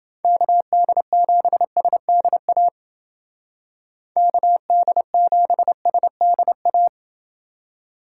Morse code, heard as a keyed tone, about 28 wpm, 700 Hz. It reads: KD7SDA KD7SDA